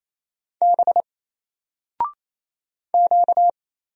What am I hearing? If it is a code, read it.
Morse code, heard as a keyed tone, about 27 wpm, 700 Hz. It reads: B Q
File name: Single-Letter-Number-28wpm